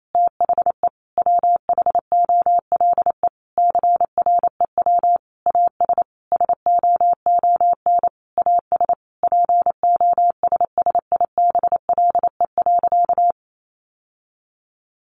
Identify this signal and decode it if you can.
Morse code, heard as a keyed tone, about 28 wpm, 700 Hz. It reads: THE WHOLE CREW AS SOON AS POSSIBLE.
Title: Sentences from Top 1500 Words 28wpm